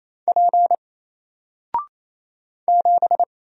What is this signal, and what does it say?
Morse code, heard as a keyed tone, about 27 wpm, 700 Hz. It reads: P 7
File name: Single-Letter-Number-28wpm